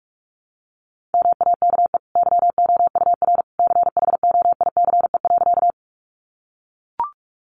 Morse code, heard as a keyed tone, about 44 wpm, 700 Hz. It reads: MAKE YOUR CHOICE.
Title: Sentences from Top 900 Words 45wpm